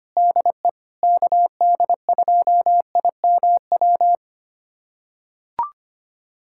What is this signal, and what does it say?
Morse code, heard as a keyed tone, about 25 wpm, 700 Hz. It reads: DE KD2IMW
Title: USA Callsign DE Callsign 25wpm